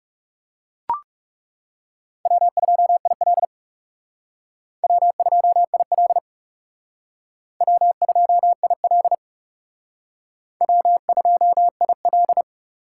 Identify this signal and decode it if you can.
Morse code, heard as a keyed tone, about 40 wpm, 700 Hz. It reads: W2IL W2IL W2IL W2IL